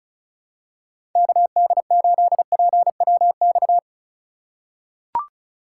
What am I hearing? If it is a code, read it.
Morse code, heard as a keyed tone, about 36 wpm, 700 Hz. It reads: KD8PWX
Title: USA Call Signs - 6 Characters - 3x Character Spacing - 35wpm